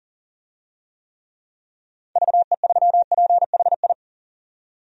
Morse code, heard as a keyed tone, about 40 wpm, 700 Hz. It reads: VE3PHI
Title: DXCC-100 Call Signs - 6 Characters - 40wpm